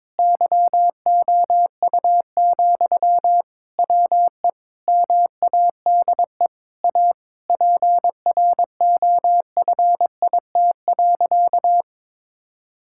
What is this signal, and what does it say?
Morse code, heard as a keyed tone, about 22 wpm, 700 Hz. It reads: YOU, WE MADE A PROFIT.
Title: Sentences from Top 1000 Words - Review Part 5 - 22wpm